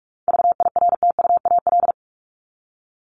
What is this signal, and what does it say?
Morse code, heard as a keyed tone, about 44 wpm, 700 Hz. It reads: VIRTUAL
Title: LY Suffix - Top 100 Words with Base Words 45wpm